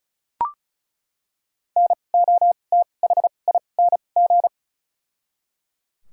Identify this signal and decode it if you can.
Morse code, heard as a keyed tone, about 36 wpm, spaced as for 25 wpm, 700 Hz. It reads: NOTHING
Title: 2x Character Spacing - Top 500 Words - 35wpm